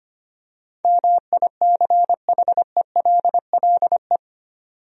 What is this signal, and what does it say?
Morse code, heard as a keyed tone, about 25 wpm, 700 Hz. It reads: MICHELLE
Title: USA Common Names - 25wpm